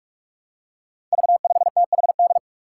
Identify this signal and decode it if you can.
Morse code, heard as a keyed tone, about 46 wpm, 700 Hz. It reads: V5THD
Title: Five Challenging Letter-Number Combinations 45wpm